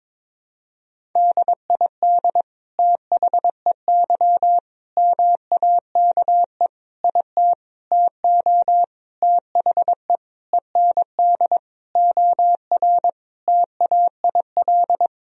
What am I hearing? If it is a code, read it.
Morse code, heard as a keyed tone, about 21 wpm, 700 Hz. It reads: DID THEY MAKE IT TO THE END OR TAIL
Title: Sentences from Top 1000 Words - Review Part 3 - 22wpm